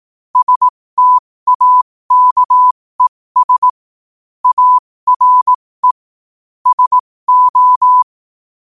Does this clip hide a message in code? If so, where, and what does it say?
Yes: Morse code from 0.4 to 8.0 s, heard as a keyed tone, about 18 wpm, spaced as for 15 wpm, 1000 Hz. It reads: STAKES ARE SO